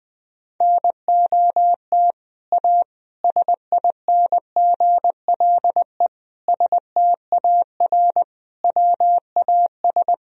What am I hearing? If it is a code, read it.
Morse code, heard as a keyed tone, about 19 wpm, 700 Hz. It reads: NOT A SINGLE STAR WAS
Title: Sentences from Top 1000 Words - Review Part 4 - 20wpm